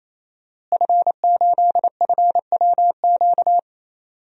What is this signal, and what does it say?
Morse code, heard as a keyed tone, about 27 wpm, 700 Hz. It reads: F8FWQ
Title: DXCC-100 Call Signs - 5 Characters - Repeated 3x - 28wpm